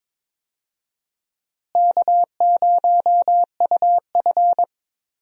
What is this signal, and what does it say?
Morse code, heard as a keyed tone, about 22 wpm, 700 Hz. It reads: K0UF
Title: USA Call Signs - 4 Characters - Repeated 3x - 22wpm